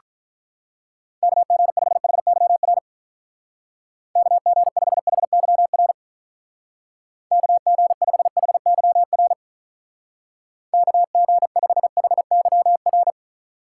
Morse code, heard as a keyed tone, about 44 wpm, 700 Hz. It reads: KG5HYR KG5HYR KG5HYR KG5HYR